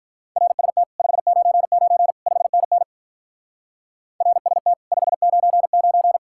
Morse code, heard as a keyed tone, about 47 wpm, 700 Hz. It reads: RST 599 5NN RST 599